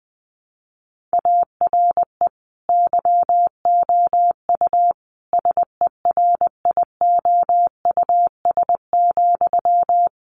Morse code, heard as a keyed tone, about 20 wpm, 700 Hz. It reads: ARE YOU SERIOUS,
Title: Sentences from Top 1100 Words 20wpm